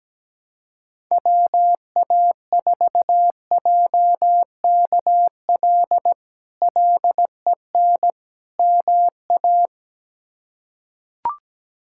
Morse code, heard as a keyed tone, about 17 wpm, 700 Hz. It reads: WA4JKL LEN MA
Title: K1USN SST QSO Practice - 17wpm